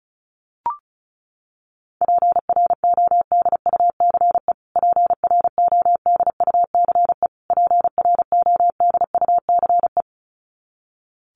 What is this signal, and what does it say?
Morse code, heard as a keyed tone, about 34 wpm, 700 Hz. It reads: PRODUCE PRODUCE PRODUCE
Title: Top 500 Words Repeated 35wpm